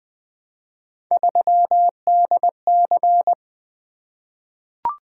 Morse code, heard as a keyed tone, about 20 wpm, 700 Hz. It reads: 3DC
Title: Three Letter-Number Combinations - 20wpm